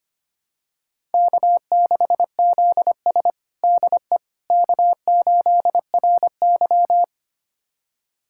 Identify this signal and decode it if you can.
Morse code, heard as a keyed tone, about 25 wpm, 700 Hz. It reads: K6ZS DE K8RY
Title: USA Callsign DE Callsign - 4 Characters - 25wpm